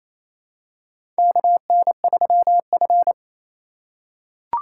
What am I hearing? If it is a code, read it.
Morse code, heard as a keyed tone, about 28 wpm, 700 Hz. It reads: KN3F
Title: USA Call Signs - 4 Characters - 8x Character Spacing - 28wpm